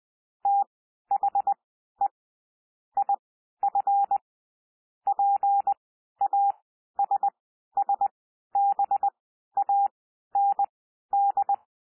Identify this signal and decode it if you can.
Morse code, heard as a keyed tone, about 20 wpm, spaced as for 12 wpm, 800 Hz. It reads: THE IF PASSBAND
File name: cwmp3 / P1 SOUND PASSAGE 20 wpm